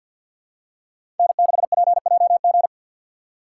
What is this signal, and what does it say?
Morse code, heard as a keyed tone, about 53 wpm, 700 Hz. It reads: N6PJG
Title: USA Call Signs - 5 Characters - 3x Character Spacing - 50wpm